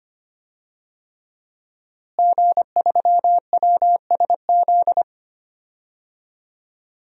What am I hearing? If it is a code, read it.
Morse code, heard as a keyed tone, about 25 wpm, 700 Hz. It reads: G3WSZ
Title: DXCC-100 Call Signs - 5 Characters - Repeated 3x - 25wpm